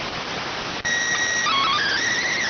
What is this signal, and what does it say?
SSTV (slow-scan television), PD 290.